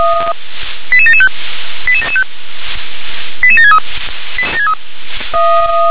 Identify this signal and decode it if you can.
Five-tone selective call (ZVEI), heard as quick runs of five tones, tones of about 70 ms. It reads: 88775 88763